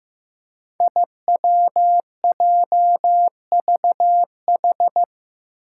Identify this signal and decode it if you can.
Morse code, heard as a keyed tone, about 15 wpm, 700 Hz. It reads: IWJVH
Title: Five Challenging Letter-Number Combinations 15wpm